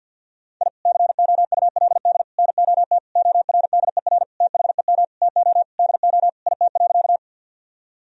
Morse code, heard as a keyed tone, about 50 wpm, 700 Hz. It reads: I COULD NOT ORDER THEM TO DO IT.
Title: Sentences from Top 1000 Words - Review Part 2 - 50wpm